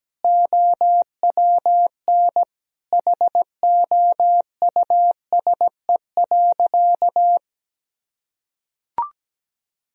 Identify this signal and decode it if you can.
Morse code, heard as a keyed tone, about 17 wpm, 700 Hz. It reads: OWN HOUSE.
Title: Sentences from Top 1000 Words - Review Part 1 - 17wpm